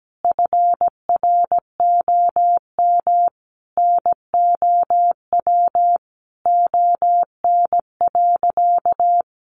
Morse code, heard as a keyed tone, about 17 wpm, 700 Hz. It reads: FROM NOW ON.
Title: Easy Phrases 17wpm